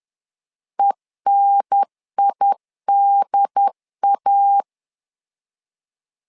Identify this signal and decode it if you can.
Morse code, heard as a keyed tone, about 11 wpm, 800 Hz. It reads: ENIDA
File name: LEZIONI / 3 MWRO / 3 M 10wpm 1828